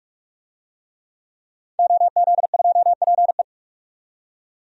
Morse code, heard as a keyed tone, about 46 wpm, 700 Hz. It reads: OZ2PE
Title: DXCC-100 Call Signs - 5 Characters - 45wpm